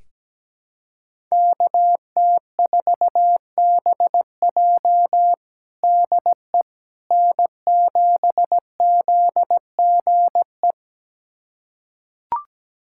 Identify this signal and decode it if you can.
Morse code, heard as a keyed tone, about 17 wpm, 700 Hz. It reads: KT4BJ DE N7ZGE
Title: USA Callsign DE Callsign - 5 Characters - 17wpm